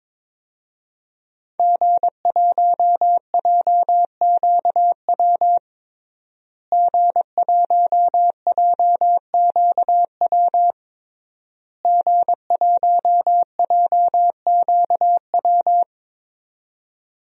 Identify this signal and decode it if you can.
Morse code, heard as a keyed tone, about 22 wpm, 700 Hz. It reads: G1JQW G1JQW G1JQW
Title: DXCC-100 Call Signs - 5 Characters - Repeated 3x - 22wpm